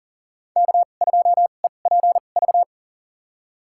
Morse code, heard as a keyed tone, about 40 wpm, spaced as for 29 wpm, 700 Hz. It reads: K2EPV
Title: USA Call Signs - 5 Characters - 2x Character Spacing - 40wpm